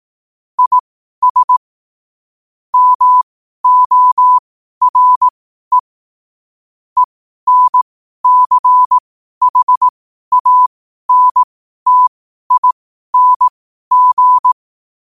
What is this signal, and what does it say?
Morse code, heard as a keyed tone, about 18 wpm, spaced as for 12 wpm, 1000 Hz. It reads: IS MORE ENCHANTING
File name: qotdfiles / 12wpm